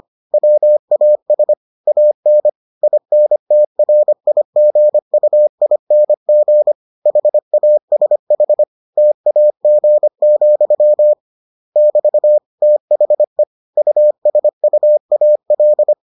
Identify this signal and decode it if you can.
Morse code, heard as a keyed tone, about 25 wpm, 600 Hz. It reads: WAS AN INTRIGUING HASH TAG, <BT> THE USUAL